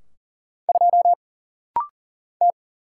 Morse code, heard as a keyed tone, about 40 wpm, 700 Hz. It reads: 2 T
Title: Single Letter-Number Alt - Rapid-Fire 40wpm